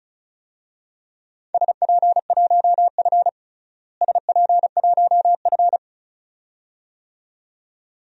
Morse code, heard as a keyed tone, about 36 wpm, 700 Hz. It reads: SP1F SP1F
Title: DXCC-100 Call Signs - 4 Characters - Repeated 2x - 35wpm